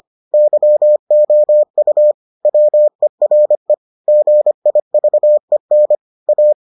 Morse code, heard as a keyed tone, about 25 wpm, 600 Hz. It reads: YOU WERE GIVEN A